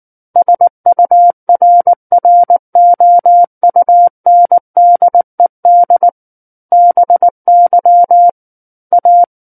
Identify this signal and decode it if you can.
Morse code, heard as a keyed tone, about 19 wpm, 700 Hz. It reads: SURROUNDED BY A
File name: Book One / 10-Chapter 10 - 19 WPM